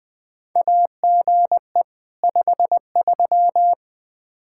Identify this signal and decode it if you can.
Morse code, heard as a keyed tone, about 20 wpm, 700 Hz. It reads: AGE 53
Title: Practice QSO - Exchange 3 - More Info - 20wpm